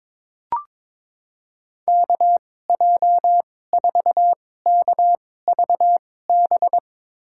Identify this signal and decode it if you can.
Morse code, heard as a keyed tone, about 22 wpm, spaced as for 16 wpm, 700 Hz. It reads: KJ4KVB
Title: USA Call Signs - 2x Character Spacing - 22wpm